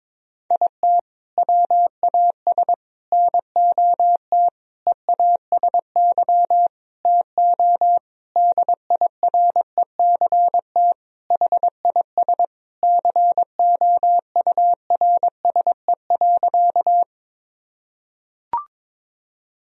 Morse code, heard as a keyed tone, about 22 wpm, 700 Hz. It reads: IT WAS NOT EASY TO DIRECT HIS COURSE.
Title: Sentences from Top 1000 Words - Review Part 2 - 22wpm